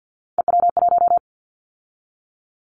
Morse code, heard as a keyed tone, about 50 wpm, 700 Hz. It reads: EW1
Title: Callsign Prefixes for Top 100 DXCC - 50wpm